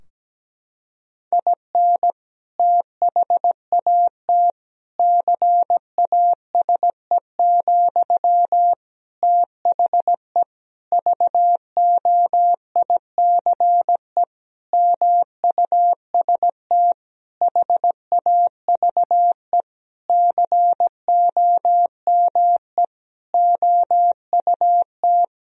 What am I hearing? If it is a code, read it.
Morse code, heard as a keyed tone, about 17 wpm, 700 Hz. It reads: IN THAT CASE, THE VOICE MUST HAVE COME OUT
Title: Sentences from Top 1000 Words - Review Part 2 - 17wpm